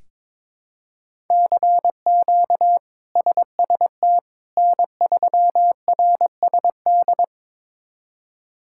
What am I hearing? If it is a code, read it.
Morse code, heard as a keyed tone, about 22 wpm, 700 Hz. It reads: CQ SST N3RSD
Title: K1USN SST QSO Practice - 22wpm